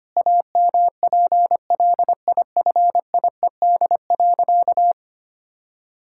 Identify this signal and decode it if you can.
Morse code, heard as a keyed tone, about 25 wpm, 700 Hz. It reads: AMPLIFIED.
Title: Pangram Challenge v2 - 25wpm